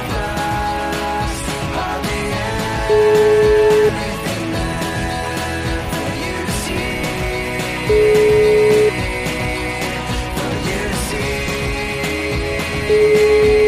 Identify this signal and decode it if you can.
Ringback tone (Europe, 425 Hz), with noise.